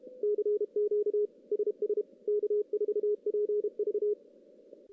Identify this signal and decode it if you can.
Morse code, heard as a keyed tone, about 32 wpm, 420 Hz. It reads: CQ SS K4PV
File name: contests / CW-SS-k5tr- -1600